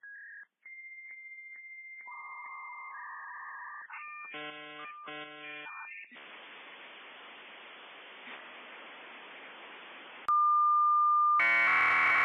Dial-up modem handshake.